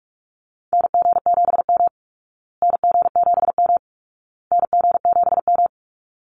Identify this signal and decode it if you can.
Morse code, heard as a keyed tone, about 44 wpm, 700 Hz. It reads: NG7M NG7M NG7M
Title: Call Signs Repeated 2x - 45wpm